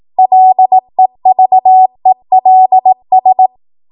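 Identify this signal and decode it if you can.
Morse code, heard as a keyed tone, about 18 wpm, 750 Hz. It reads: LEVELS
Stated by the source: Code practice at 18 WPM, a transition run bridging 15 to 20 WPM.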